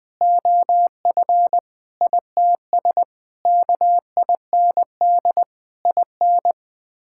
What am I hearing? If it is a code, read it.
Morse code, heard as a keyed tone, about 20 wpm, 700 Hz. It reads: OF ITS KIND IN
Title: Sets of 6 Words - 20wpm